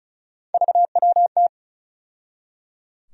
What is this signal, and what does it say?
Morse code, heard as a keyed tone, about 34 wpm, 700 Hz. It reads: VWT
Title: USA Call Sign Postfixes - 3 Letters 35wpm